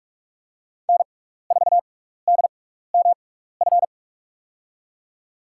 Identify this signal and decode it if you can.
Morse code, heard as a keyed tone, about 44 wpm, spaced as for 15 wpm, 690 Hz. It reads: N4DMF
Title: USA Call Signs - 5 Characters - 6x Character Spacing - 45wpm